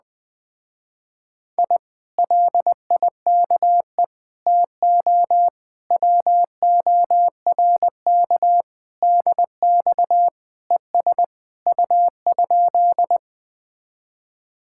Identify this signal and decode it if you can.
Morse code, heard as a keyed tone, about 20 wpm, 700 Hz. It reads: I LIKE TO WORK DX ES U?
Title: Speed-Racing Common QSO Phrases Encore - 17wpm